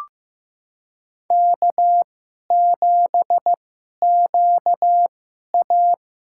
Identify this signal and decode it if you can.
Morse code, heard as a keyed tone, about 15 wpm, spaced as for 11 wpm, 700 Hz. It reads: K7QA
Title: USA Call Signs - 4 Characters - 2x Character Spacing - 15wpm